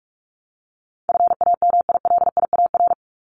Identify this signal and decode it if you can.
Morse code, heard as a keyed tone, about 44 wpm, 700 Hz. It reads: FAMILIAR
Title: Top 751-1000 Words 45wpm